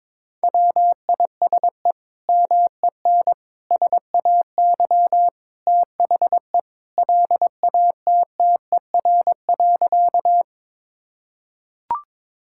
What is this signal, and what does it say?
Morse code, heard as a keyed tone, about 22 wpm, 700 Hz. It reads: WISE MEN SAY THE LATTER.
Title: Sentences from Top 1600 Words 22wpm